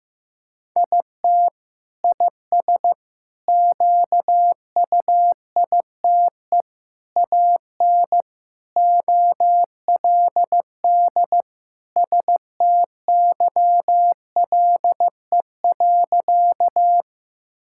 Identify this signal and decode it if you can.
Morse code, heard as a keyed tone, about 15 wpm, 700 Hz. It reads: IT IS QUITE AN OLD STYLE.
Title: Sentences from Top 1000 Words - Review Part 5 - 15wpm